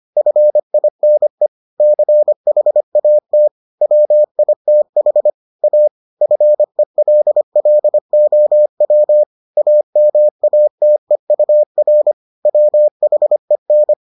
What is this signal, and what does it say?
Morse code, heard as a keyed tone, about 25 wpm, 600 Hz. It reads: FINE CHAT WITH A FELLOW AMATEUR WHEN